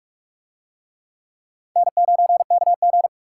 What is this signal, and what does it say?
Morse code, heard as a keyed tone, about 44 wpm, 700 Hz. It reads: N9KG